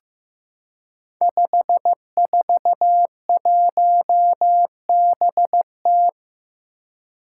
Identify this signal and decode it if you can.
Morse code, heard as a keyed tone, about 15 wpm, 700 Hz. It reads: 541BT
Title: Five Challenging Letter-Number Combinations 15wpm